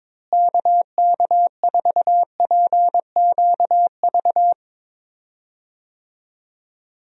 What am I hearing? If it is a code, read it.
Morse code, heard as a keyed tone, about 22 wpm, 700 Hz. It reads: KK4PQV
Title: USA Call Signs - 6 Characters - Repeated 2x - 22wpm